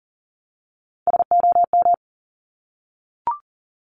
Morse code, heard as a keyed tone, about 38 wpm, 690 Hz. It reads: SOM
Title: Top Three Letter Combinations in Words 40wpm